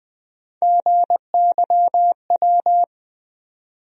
Morse code, heard as a keyed tone, about 20 wpm, 700 Hz. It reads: GYW
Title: USA Call Sign Postfixes - 3 Letters 20wpm